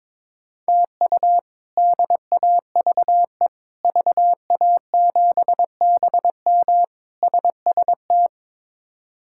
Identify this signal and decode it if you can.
Morse code, heard as a keyed tone, about 22 wpm, 700 Hz. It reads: TU DAVE VA7BM SST
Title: K1USN SST QSO Practice - 22wpm